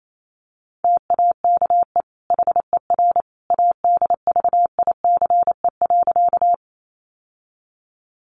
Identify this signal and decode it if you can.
Morse code, heard as a keyed tone, about 28 wpm, 700 Hz. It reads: TAKE HER ADVICE.
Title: Sentences from Top 1100 Words 28wpm